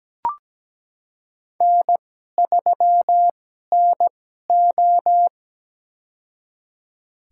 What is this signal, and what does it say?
Morse code, heard as a keyed tone, about 17 wpm, spaced as for 12 wpm, 700 Hz. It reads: N3NO
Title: USA Call Signs - 4 Characters - 2x Character Spacing - 17wpm